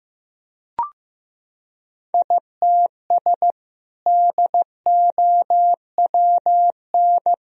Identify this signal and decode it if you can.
Morse code, heard as a keyed tone, about 15 wpm, 700 Hz. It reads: ITS DOWN
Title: Sentences from Top 1000 Words - Review Part 2 - 15wpm